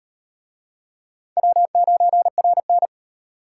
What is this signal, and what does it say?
Morse code, heard as a keyed tone, about 38 wpm, 700 Hz. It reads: W9RN